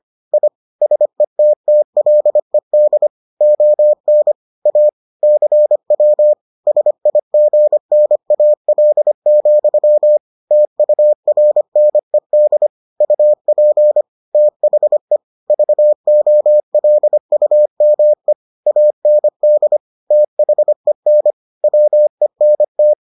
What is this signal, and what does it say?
Morse code, heard as a keyed tone, about 25 wpm, 600 Hz. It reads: I SETTLED ON A CW SIGNAL, TURNED UP THE VOLUME AND THEN WENT